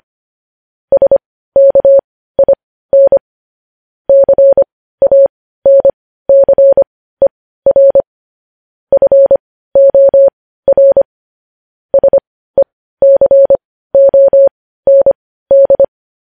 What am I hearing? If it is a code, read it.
Morse code, heard as a keyed tone, about 25 wpm, spaced as for 15 wpm, 550 Hz. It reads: SKIN CANCER FOR SECOND
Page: News Headlines in Morse Code at 15 WPM